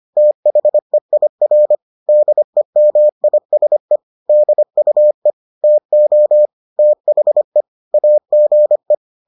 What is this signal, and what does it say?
Morse code, heard as a keyed tone, about 25 wpm, 600 Hz. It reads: THEIR DEMISE DUE TO THE AGE